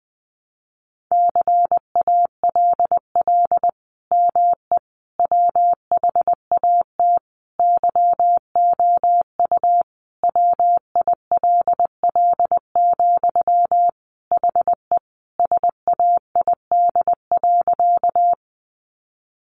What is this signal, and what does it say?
Morse code, heard as a keyed tone, about 20 wpm, 700 Hz. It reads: CALL ME WHAT YOU WILL, HE SAID.
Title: Sentences from Top 200 Words 20wpm